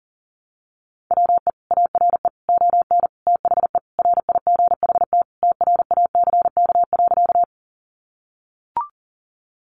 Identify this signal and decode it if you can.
Morse code, heard as a keyed tone, about 40 wpm, 700 Hz. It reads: WE ARE ON THE RIGHT TRACK.
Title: Sentences from Top 1100 Words 40wpm